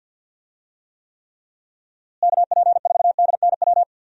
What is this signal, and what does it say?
Morse code, heard as a keyed tone, about 50 wpm, 700 Hz. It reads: KP4DNW
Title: USA Call Signs - 6 Characters - Repeated 3x - 50wpm